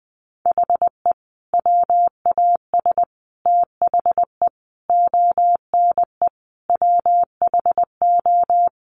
Morse code, heard as a keyed tone, about 19 wpm, 700 Hz. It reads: HE WAS THE ONE WHO
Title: Sets of 5 Words Encore2 20wpm